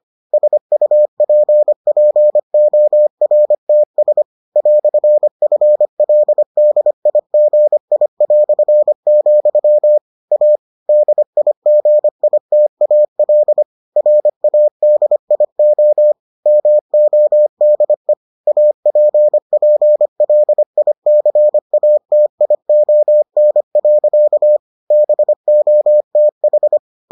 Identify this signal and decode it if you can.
Morse code, heard as a keyed tone, about 25 wpm, 600 Hz. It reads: SUPPORTS "FLDIGI", A DIGITAL RADIO MODE APPLICATION. BOTH